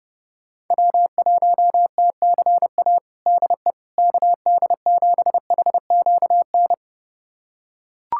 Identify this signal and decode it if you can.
Morse code, heard as a keyed tone, about 30 wpm, 690 Hz. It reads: W1TCA DE KD7HQN
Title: USA Callsign DE Callsign 30wpm